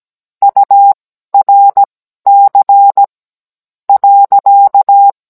Morse code, heard as a keyed tone, about 17 wpm, spaced as for 12 wpm, 800 Hz. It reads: URC .
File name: Cours-NR19-16mm